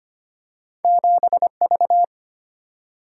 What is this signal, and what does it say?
Morse code, heard as a keyed tone, about 25 wpm, 700 Hz. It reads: 7 V
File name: Two Letter-Number Combinations - 25wpm